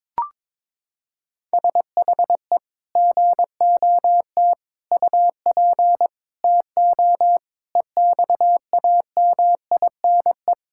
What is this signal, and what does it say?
Morse code, heard as a keyed tone, about 22 wpm, 700 Hz. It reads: SHE GOT UP TO EXAMINE
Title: Sentences from Top 1600 Words 22wpm